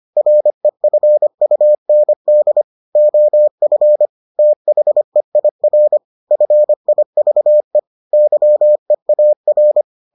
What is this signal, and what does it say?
Morse code, heard as a keyed tone, about 25 wpm, 600 Hz. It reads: REFUND OF THEIR FIVE YEAR